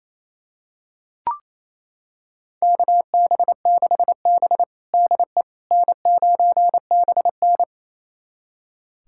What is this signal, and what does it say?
Morse code, heard as a keyed tone, about 28 wpm, 700 Hz. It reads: KB6B DE N9BN
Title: USA Callsign DE Callsign - 4 Characters - 28wpm